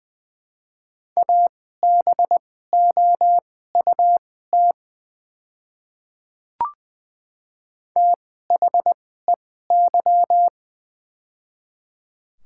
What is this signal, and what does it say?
Morse code, heard as a keyed tone, about 20 wpm, spaced as for 14 wpm, 700 Hz. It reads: ABOUT THEY
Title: 2x Character Spacing - Top 300 Words - 20wpm